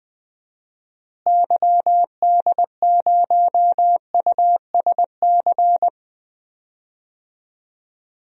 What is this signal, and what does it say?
Morse code, heard as a keyed tone, about 20 wpm, 700 Hz. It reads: YD0USC
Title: DXCC-100 Call Signs - 6 Characters - Repeated 2x - 20wpm